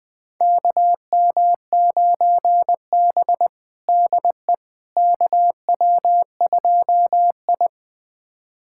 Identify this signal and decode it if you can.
Morse code, heard as a keyed tone, about 20 wpm, 700 Hz. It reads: KM9B DE KW2I
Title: USA Callsign DE Callsign - 4 Characters - 20wpm